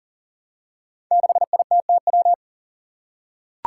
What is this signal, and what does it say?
Morse code, heard as a keyed tone, about 40 wpm, 700 Hz. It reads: 6ITTW
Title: Five Challenging Letter-Number Combinations 40wpm